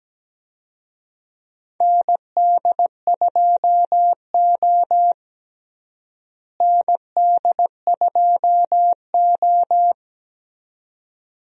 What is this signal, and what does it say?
Morse code, heard as a keyed tone, about 17 wpm, 700 Hz. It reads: ND2O ND2O
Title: USA Call Signs - 4 Characters - Repeated 2x - 17wpm